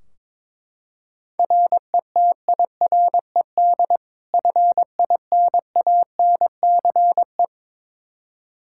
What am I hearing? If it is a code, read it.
Morse code, heard as a keyed tone, about 22 wpm, 700 Hz. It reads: RETIRED FINANCE
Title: Practice QSO - Exchange 3 - More Info - 22wpm